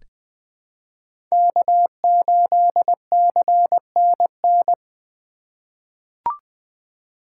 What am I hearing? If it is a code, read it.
Morse code, heard as a keyed tone, about 21 wpm, 700 Hz. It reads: K8CNN
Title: USA Call Signs - 5 Characters - 4x Character Spacing - 20wpm